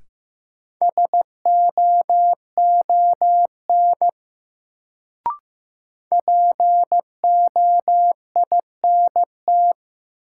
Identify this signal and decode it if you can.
Morse code, heard as a keyed tone, about 15 wpm, 700 Hz. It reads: SOON POINT
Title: Sets of 3 Words - New Word Drill - 15wpm